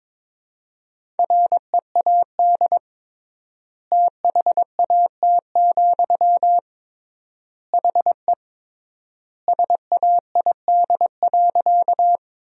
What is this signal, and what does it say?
Morse code, heard as a keyed tone, about 21 wpm, 700 Hz. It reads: READ THAT, HE SAID.
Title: 3x Word-Spacing - Sentences from Top 200 Words - 22wpm